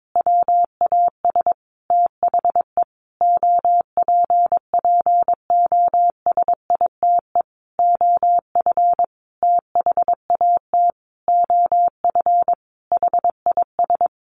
Morse code, heard as a keyed tone, about 22 wpm, 700 Hz. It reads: WAS THE OPPOSITE OF THAT OF HIS
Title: Sentences from Top 1300 Words 22wpm